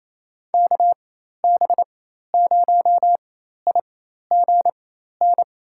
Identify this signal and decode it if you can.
Morse code, heard as a keyed tone, about 28 wpm, spaced as for 13 wpm, 700 Hz. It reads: KB0IGN
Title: USA Call Signs - 4x Character Spacing - 28wpm